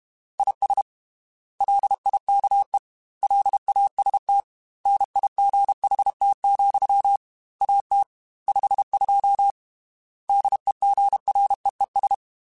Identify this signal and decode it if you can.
Morse code, heard as a keyed tone, about 32 wpm, 800 Hz. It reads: IS LIKE LAST NIGHT, AT 52 DEGREES
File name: Weather Sets / Weather Sets-mp3s / Weather 4 17 @30